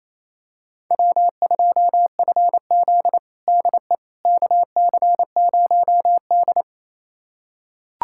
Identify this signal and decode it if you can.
Morse code, heard as a keyed tone, about 28 wpm, 700 Hz. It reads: W2FZ DE KC0D
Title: USA Callsign DE Callsign 28wpm